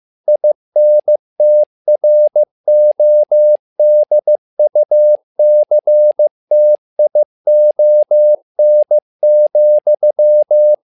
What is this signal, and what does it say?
Morse code, heard as a keyed tone, about 15 wpm, 600 Hz. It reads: INTRODUCTION,
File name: learning center / tower trouble / tower trouble 15wpm